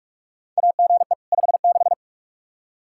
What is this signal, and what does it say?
Morse code, heard as a keyed tone, about 46 wpm, 700 Hz. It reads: AGE 56
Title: K1USN SST QSO Practice 45wpm